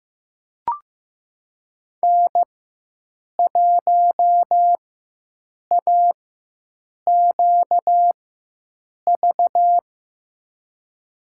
Morse code, heard as a keyed tone, about 15 wpm, spaced as for 7 wpm, 700 Hz. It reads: N1AQV
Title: USA Call Signs - 5 Characters - 4x Character Spacing - 15wpm